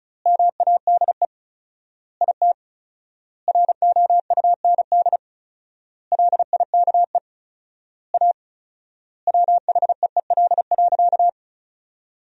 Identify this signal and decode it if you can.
Morse code, heard as a keyed tone, about 36 wpm, 700 Hz. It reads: MADE IT ROUND LIKE A WHEEL.
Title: 4x Word-Spacing - Sentences from Top 500 Words - 35wpm